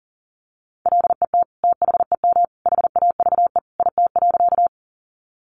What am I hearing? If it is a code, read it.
Morse code, heard as a keyed tone, about 40 wpm, 700 Hz. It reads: LET THEM HAVE IT.
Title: Sentences from Top 300 Words 40wpm